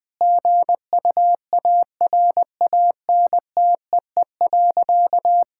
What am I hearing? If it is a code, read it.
Morse code, heard as a keyed tone, about 20 wpm, 700 Hz. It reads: GUARANTEE.
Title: Sentences from Top 1600 Words 20wpm